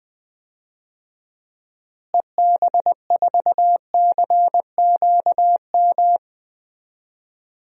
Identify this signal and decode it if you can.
Morse code, heard as a keyed tone, about 20 wpm, 700 Hz. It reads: EB4CQM
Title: DXCC-100 Call Signs - 6 Characters - Repeated 3x - 20wpm